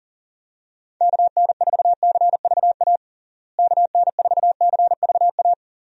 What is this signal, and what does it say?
Morse code, heard as a keyed tone, about 40 wpm, 700 Hz. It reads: KN4CVA KN4CVA
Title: USA Call Signs - 6 Characters - Repeated 2x - 40wpm